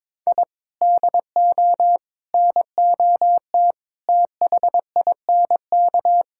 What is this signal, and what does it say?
Morse code, heard as a keyed tone, about 22 wpm, 700 Hz. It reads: I DO NOT THINK
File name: practice / 22wpm / Sentences with 4-Word Phrases - 22wpm